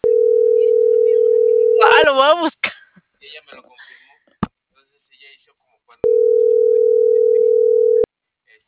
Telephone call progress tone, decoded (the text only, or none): ringback tone